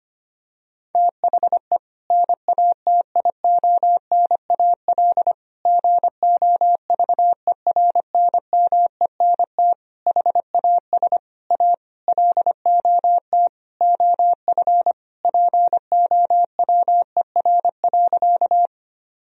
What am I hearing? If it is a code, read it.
Morse code, heard as a keyed tone, about 25 wpm, 700 Hz. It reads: THE NATIONAL GOVERNMENT HAS A LOT OF POWER.
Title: Sentences from Top 1000 Words - Review Part 4 - 25wpm